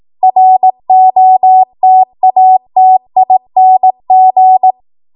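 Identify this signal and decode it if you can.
Morse code, heard as a keyed tone, about 18 wpm, 750 Hz. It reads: ROTATING
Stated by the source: ARRL 18 wpm Morse code practice archive transmission.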